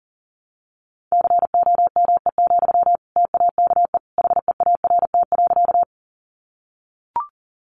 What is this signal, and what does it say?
Morse code, heard as a keyed tone, about 40 wpm, 700 Hz. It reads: COME, TAKE HEART.
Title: Sentences from Top 1200 Words 40wpm